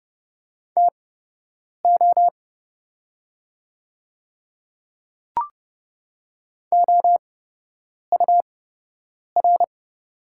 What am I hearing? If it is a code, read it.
Morse code, heard as a keyed tone, about 29 wpm, spaced as for 8 wpm, 700 Hz. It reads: TO OUR
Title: 8x Character Spacing - Top 200 Words - 30wpm